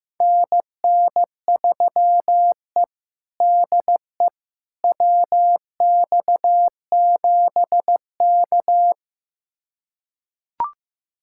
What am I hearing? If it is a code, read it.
Morse code, heard as a keyed tone, about 15 wpm, 700 Hz. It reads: NN3E DE WX7K
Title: USA Callsign DE Callsign - 4 Characters - 15wpm